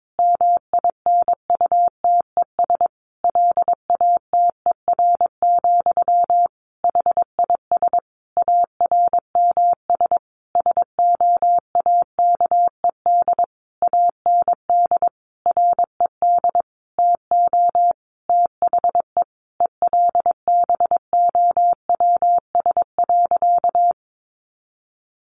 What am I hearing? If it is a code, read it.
Morse code, heard as a keyed tone, about 22 wpm, 700 Hz. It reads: MINUTES LATER, HIS ARMS SOAKED AND RED TO THE ELBOWS.
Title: A Sound Of Thunder 22wpm